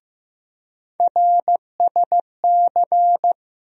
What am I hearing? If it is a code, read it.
Morse code, heard as a keyed tone, about 15 wpm, 700 Hz. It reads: RSC
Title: USA Call Sign Postfixes - 3 Letters 15wpm